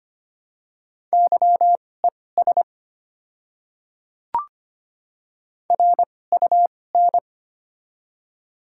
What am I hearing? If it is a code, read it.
Morse code, heard as a keyed tone, about 25 wpm, spaced as for 18 wpm, 700 Hz. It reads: YES RUN
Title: 2x Character Spacing - Top 500 Words - 25wpm